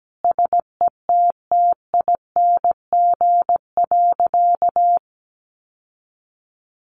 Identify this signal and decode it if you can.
Morse code, heard as a keyed tone, about 17 wpm, 700 Hz. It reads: SETTING.
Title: Sentences from Top 1300 Words 17wpm